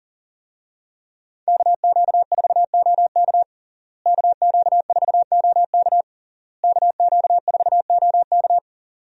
Morse code, heard as a keyed tone, about 37 wpm, 700 Hz. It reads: KQ4OK KQ4OK KQ4OK
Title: USA Call Signs - 5 Characters - Repeated 3x - 40wpm